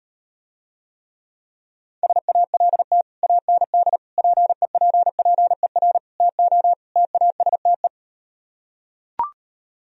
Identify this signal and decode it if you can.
Morse code, heard as a keyed tone, about 38 wpm, 700 Hz. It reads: SALT AND PEPPER TO TASTE